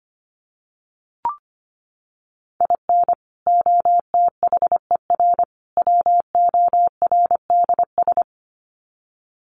Morse code, heard as a keyed tone, about 25 wpm, 700 Hz. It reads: IN OTHER WORDS
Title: Sets of 3 Words 25wpm